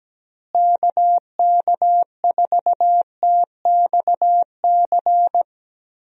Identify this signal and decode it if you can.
Morse code, heard as a keyed tone, about 17 wpm, 700 Hz. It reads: KK4TXC
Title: USA Call Signs - 6 Characters - Repeated 3x - 17wpm